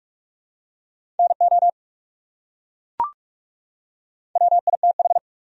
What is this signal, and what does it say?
Morse code, heard as a keyed tone, about 37 wpm, 700 Hz. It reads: NO WITH
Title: Speed-Racing Top 200 Words 30wpm